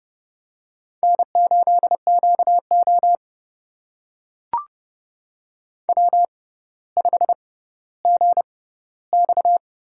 Morse code, heard as a keyed tone, about 30 wpm, spaced as for 15 wpm, 700 Hz. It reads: N8QO W5GX
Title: USA Call Signs - 4 Characters - 6x Character Spacing - 30wpm